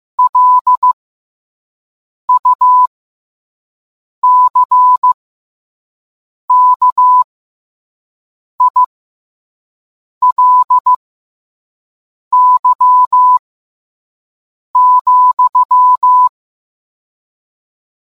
Morse code, heard as a keyed tone, about 15 wpm, spaced as for 5 wpm, 1000 Hz. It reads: LUCKILY,